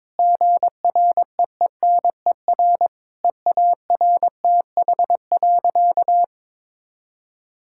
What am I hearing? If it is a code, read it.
Morse code, heard as a keyed tone, about 21 wpm, 700 Hz. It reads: GREENER EARTH.
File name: Difficult Sentences with Hints - 22wpm